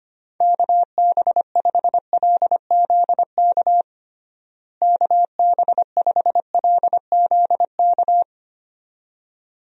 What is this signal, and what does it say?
Morse code, heard as a keyed tone, about 25 wpm, 700 Hz. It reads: KB5LZK KB5LZK
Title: USA Call Signs - 6 Characters - Repeated 2x - 25wpm